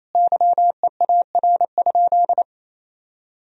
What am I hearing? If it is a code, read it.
Morse code, heard as a keyed tone, about 28 wpm, 700 Hz. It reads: YEAR?
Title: Sentences from Top 1000 Words - Review Part 4 - 28wpm